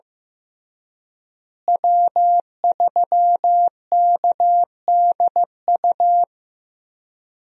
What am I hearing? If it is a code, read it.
Morse code, heard as a keyed tone, about 15 wpm, 700 Hz. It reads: W3KDU
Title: USA Call Signs - 5 Characters - Repeated 3x - 15wpm